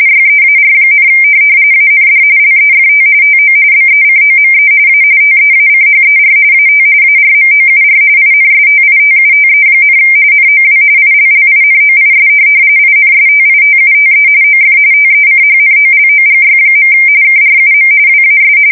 RTTY, about 45.45 baud. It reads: YEAR, AND LEFT TO BECOME A TEACHER AT ARMOUR INSTITUTE, NOW ILLINOIS INSTITUTE OF TECHNOLOGY. KRUM WAS
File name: fsk170 / teletype-corp